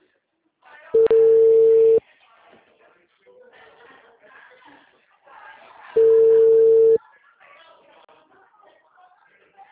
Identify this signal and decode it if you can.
Ringback tone (China, 450 Hz), with speech.